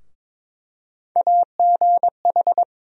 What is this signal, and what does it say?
Morse code, heard as a keyed tone, about 21 wpm, 700 Hz. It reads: AGH
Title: USA Call Sign Postfixes - 3 Letters 22wpm